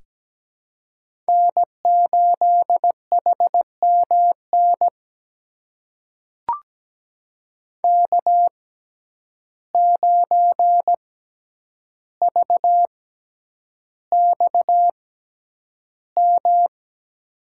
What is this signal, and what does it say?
Morse code, heard as a keyed tone, about 17 wpm, spaced as for 9 wpm, 700 Hz. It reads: N8HMN K9VXM
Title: USA Call Signs - 5 Characters - 6x Character Spacing - 17wpm